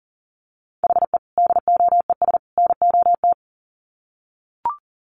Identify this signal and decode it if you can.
Morse code, heard as a keyed tone, about 40 wpm, 700 Hz. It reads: HE DOES NOT
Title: Sets of 3 Words 40wpm